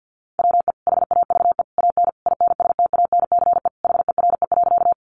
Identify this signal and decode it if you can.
Morse code, heard as a keyed tone, about 50 wpm, 700 Hz. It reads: WE HAVE AN INSTANCE HERE.
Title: Sentences from Top 800 Words 50wpm